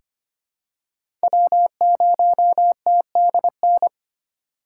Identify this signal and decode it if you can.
Morse code, heard as a keyed tone, about 25 wpm, 700 Hz. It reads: W0TDN
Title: USA Call Signs - 5 Characters - 3x Character Spacing - 25wpm